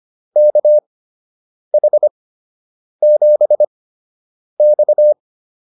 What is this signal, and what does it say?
Morse code, heard as a keyed tone, about 25 wpm, spaced as for 8 wpm, 600 Hz. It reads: KH7X